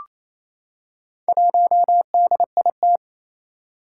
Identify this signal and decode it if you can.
Morse code, heard as a keyed tone, about 28 wpm, 700 Hz. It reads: 1DIT
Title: Four Challenging Letter-Number Combinations 28wpm